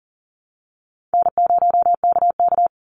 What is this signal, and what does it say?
Morse code, heard as a keyed tone, about 40 wpm, 700 Hz. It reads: N0KK
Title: Easy Call Signs Repeated 3x - 40wpm